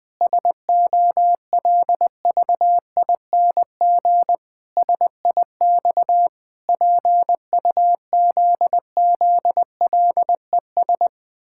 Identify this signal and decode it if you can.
Morse code, heard as a keyed tone, about 20 wpm, 700 Hz. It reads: SOLVING SIX PUZZLES
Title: Pangram Challenge v2 - 20wpm